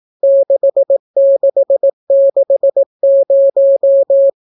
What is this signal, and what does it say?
Morse code, heard as a keyed tone, about 18 wpm, 550 Hz. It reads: 6660